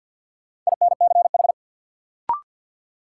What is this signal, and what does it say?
Morse code, heard as a keyed tone, about 51 wpm, 700 Hz. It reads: INCH
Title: Sentences from Top 1000 Words - Review Part 3 - New Word Drill - 50wpm